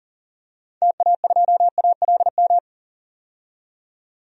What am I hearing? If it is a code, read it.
Morse code, heard as a keyed tone, about 40 wpm, 700 Hz. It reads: TA2ALM
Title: DXCC-100 Call Signs - 6 Characters - 40wpm